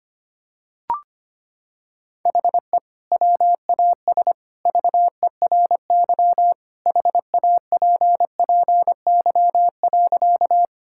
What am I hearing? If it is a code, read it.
Morse code, heard as a keyed tone, about 25 wpm, 700 Hz. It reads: HE WAS VERY HAPPY.
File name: practice / 25wpm / Sentences with 2-Word Phrases - 25wpm